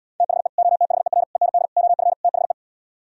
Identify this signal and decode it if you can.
Morse code, heard as a keyed tone, about 77 wpm, 700 Hz. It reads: THE OTHER END OF THE